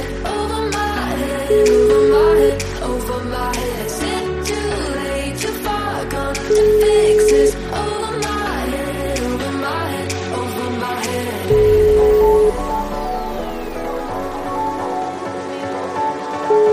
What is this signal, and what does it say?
Ringback tone (Europe, 425 Hz), with noise.